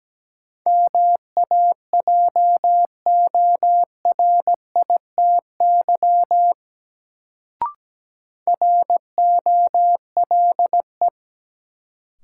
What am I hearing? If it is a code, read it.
Morse code, heard as a keyed tone, about 17 wpm, 700 Hz. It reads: MAJORITY ROLE
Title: Sets of 3 Words - New Word Drill - 17wpm